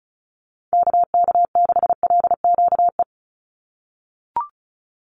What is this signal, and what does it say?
Morse code, heard as a keyed tone, about 34 wpm, 700 Hz. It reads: KK6LQE
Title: USA Call Signs 35wpm